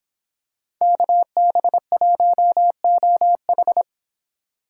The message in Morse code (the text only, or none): KB1OH